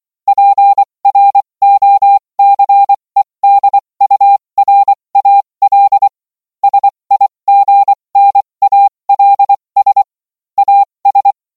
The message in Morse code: PROCEDURAL SIGNALS AS